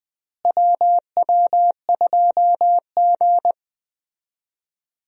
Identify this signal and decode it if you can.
Morse code, heard as a keyed tone, about 20 wpm, 700 Hz. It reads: WW2G
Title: USA Call Signs - 4 Characters - Repeated 2x - 20wpm